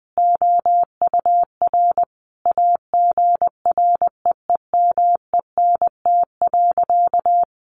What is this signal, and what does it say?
Morse code, heard as a keyed tone, about 20 wpm, 700 Hz. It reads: OUR AGREEMENT.
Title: Sentences from Top 1500 Words 20wpm